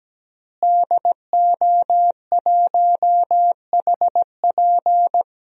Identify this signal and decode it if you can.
Morse code, heard as a keyed tone, about 17 wpm, 700 Hz. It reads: DO1HP
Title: DXCC-100 Call Signs - 5 Characters - Repeated 2x - 17wpm